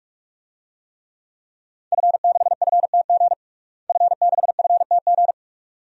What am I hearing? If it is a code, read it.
Morse code, heard as a keyed tone, about 46 wpm, 700 Hz. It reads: F6FTG F6FTG
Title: DXCC-100 Call Signs - 5 Characters - Repeated 2x - 45wpm